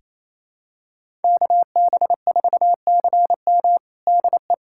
Morse code, heard as a keyed tone, about 28 wpm, 700 Hz. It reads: KB4CM DE
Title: USA Callsign DE Callsign - 5 Characters - 28wpm